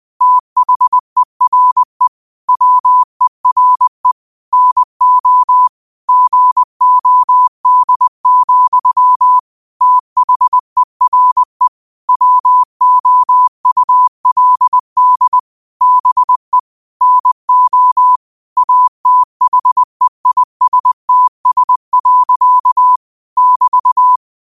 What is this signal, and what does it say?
Morse code, heard as a keyed tone, about 20 wpm, 1000 Hz. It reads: THERE WERE NO GOD, THERE WOULD BE NO ATHEISTS. <BT>
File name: qotdfiles / 20wpm